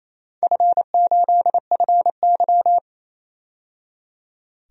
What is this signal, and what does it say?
Morse code, heard as a keyed tone, about 28 wpm, 700 Hz. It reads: F8FY
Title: DXCC-100 Call Signs - 4 Characters - 28wpm